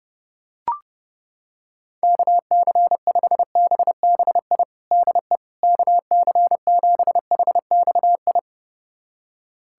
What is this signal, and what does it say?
Morse code, heard as a keyed tone, about 30 wpm, 700 Hz. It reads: KC5BBI DE KC7HXI
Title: USA Callsign DE Callsign - 6 Characters - 30wpm